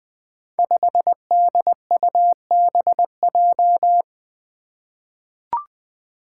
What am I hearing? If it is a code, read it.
Morse code, heard as a keyed tone, about 20 wpm, 700 Hz. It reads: 5DUBJ
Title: Five Challenging Letter-Number Combinations 20wpm